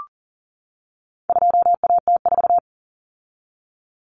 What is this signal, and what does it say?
Morse code, heard as a keyed tone, about 40 wpm, 700 Hz. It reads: 2AT4
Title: Common Cut Numbers - 4-Digit 40wpm